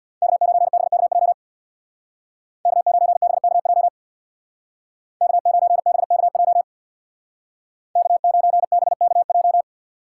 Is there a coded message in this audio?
Yes: Morse code from 0.2 to 9.6 s, heard as a keyed tone, about 65 wpm, 700 Hz. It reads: K9BKJ K9BKJ K9BKJ K9BKJ